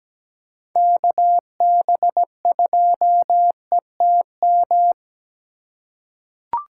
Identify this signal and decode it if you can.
Morse code, heard as a keyed tone, about 17 wpm, 700 Hz. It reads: KB2ETM
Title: USA Call Signs - 6 Characters - 3x Character Spacing - 17wpm